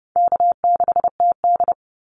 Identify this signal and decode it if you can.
Morse code, heard as a keyed tone, about 30 wpm, 700 Hz. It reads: K6TD
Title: Easy Call Signs - 30wpm